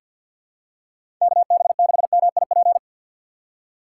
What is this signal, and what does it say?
Morse code, heard as a keyed tone, about 53 wpm, 700 Hz. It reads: KB6MIP
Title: USA Call Signs - 6x Character Spacing - 50wpm